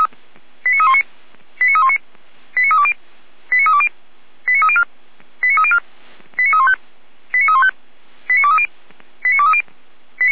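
Five-tone selective call (ZVEI), heard as quick runs of five tones, tones of about 70 ms. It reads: 89319 89319 89320 89320 89304 89304 89315 89315 89230 89230
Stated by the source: ZVEI five-tone selective call received on SDR.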